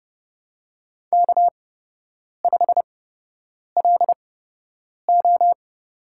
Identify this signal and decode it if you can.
Morse code, heard as a keyed tone, about 29 wpm, spaced as for 8 wpm, 700 Hz. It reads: K5LO
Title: USA Call Signs - 4 Characters - 8x Character Spacing - 30wpm